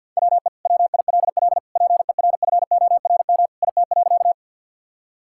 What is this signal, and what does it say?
Morse code, heard as a keyed tone, about 50 wpm, 700 Hz. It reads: WE WILL PERFORM IT.
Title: Sentences from Top 1000 Words - Review Part 5 - 50wpm